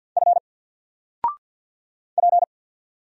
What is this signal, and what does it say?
Morse code, heard as a keyed tone, about 50 wpm, 700 Hz. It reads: F P
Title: Single Letter-Number 50wpm